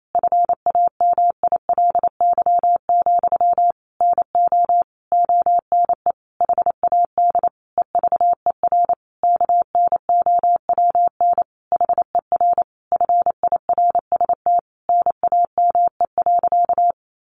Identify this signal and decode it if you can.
Morse code, heard as a keyed tone, about 28 wpm, 700 Hz. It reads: FAMILY, NO ONE HAD EVER KNOWN HER FIRST NAME.
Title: Sentences from Top 1200 Words 28wpm